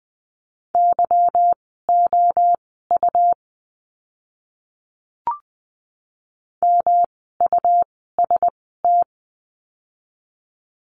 Morse code, heard as a keyed tone, about 20 wpm, spaced as for 14 wpm, 700 Hz. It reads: YOU MUST
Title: 237 Lesson 38f - Letter q - Words 20wpm